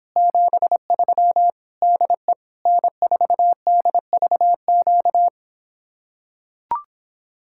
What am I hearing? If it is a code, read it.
Morse code, heard as a keyed tone, about 26 wpm, 700 Hz. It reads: 73 DE N4DVQ